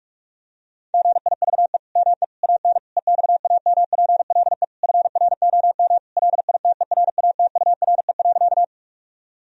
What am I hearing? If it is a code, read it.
Morse code, heard as a keyed tone, about 46 wpm, 700 Hz. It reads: GIVE ME AN EXAMPLE FROM LITERATURE.
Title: Sentences from Top 1000 Words - Review Part 3 - 45wpm